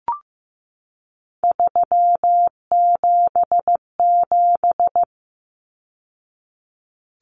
Morse code, heard as a keyed tone, about 15 wpm, 700 Hz. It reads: 377
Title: Numbers - 3-Digit 15wpm